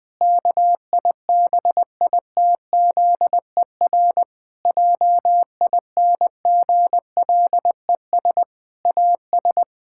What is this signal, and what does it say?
Morse code, heard as a keyed tone, about 20 wpm, 700 Hz. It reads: KIBITZER JINGLES AS
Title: Pangram Challenge v2 - 20wpm